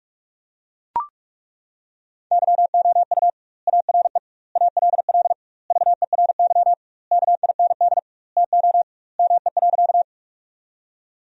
Morse code, heard as a keyed tone, about 41 wpm, 700 Hz. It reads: YOU ARE ALL VERY KIND TO ME.
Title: 2x Word-Spacing - Sentences from Top 200 Words - 45wpm